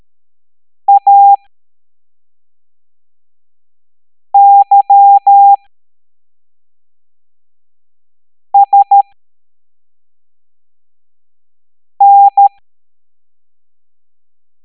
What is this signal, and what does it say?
Morse code, heard as a keyed tone, about 13 wpm, spaced as for 3 wpm, 800 Hz. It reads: AYSN
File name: hamdata / L05-EX01-SVY